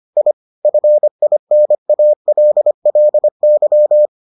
Morse code, heard as a keyed tone, about 25 wpm, 600 Hz. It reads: I FINALLY